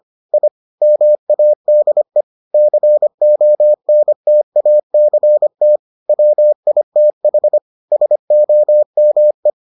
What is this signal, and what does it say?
Morse code, heard as a keyed tone, about 25 wpm, 600 Hz. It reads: I MADE CONTACT WITH SOME